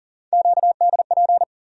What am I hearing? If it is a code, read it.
Morse code, heard as a keyed tone, about 40 wpm, 700 Hz. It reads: QDP
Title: USA Call Sign Postfixes - 3 Letters 40wpm